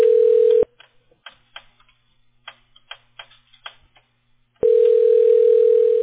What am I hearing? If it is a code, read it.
Ringback tone (North America), 440 + 480 Hz.